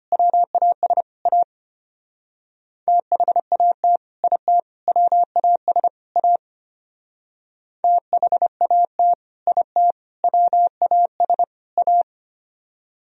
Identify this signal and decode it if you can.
Morse code, heard as a keyed tone, about 30 wpm, 690 Hz. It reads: WAS A THAT IT WAS A THAT IT WAS A
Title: Speed-Racing - Sets of 4 Words 25wpm